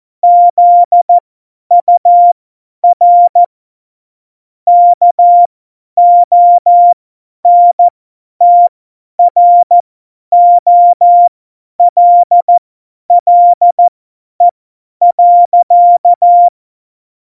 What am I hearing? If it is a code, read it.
Morse code, heard as a keyed tone, about 14 wpm, spaced as for 10 wpm, 700 Hz. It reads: ZUR KONTROLLE.
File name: Morsekurs / Witze / Witz 14 Juli 2020, 50 F70